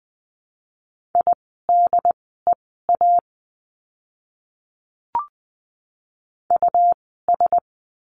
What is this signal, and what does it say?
Morse code, heard as a keyed tone, about 20 wpm, spaced as for 15 wpm, 700 Hz. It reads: IDEA US
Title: 108 Lesson 18f - Letter m - Words 20wpm